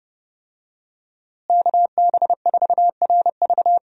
Morse code, heard as a keyed tone, about 30 wpm, 700 Hz. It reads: KB4RV
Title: USA Call Signs - 5 Characters - Repeated 3x - 30wpm